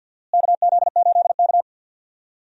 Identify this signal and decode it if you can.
Morse code, heard as a keyed tone, about 47 wpm, 700 Hz. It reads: KZ8K
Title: USA Call Signs - 4 Characters - Repeated 3x - 50wpm